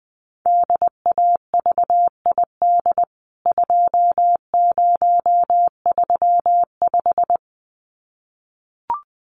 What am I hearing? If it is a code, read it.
Morse code, heard as a keyed tone, about 20 wpm, 700 Hz. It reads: DAVID 2035
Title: CWT Exchange 20wpm